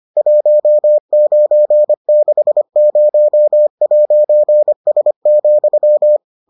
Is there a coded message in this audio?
Yes: Morse code from 0.2 to 6.2 s, heard as a keyed tone, about 25 wpm, 600 Hz. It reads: 1960'S,